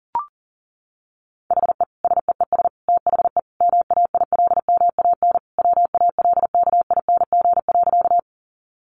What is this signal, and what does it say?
Morse code, heard as a keyed tone, about 40 wpm, 700 Hz. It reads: HE SEES THE MAILMAN WALKING.
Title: Easy Phrases 40wpm